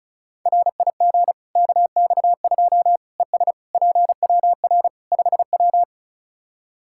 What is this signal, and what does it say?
Morse code, heard as a keyed tone, about 36 wpm, 700 Hz. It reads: RIG KX2 ES PWR 5W
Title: K1USN SST QSO Practice 35wpm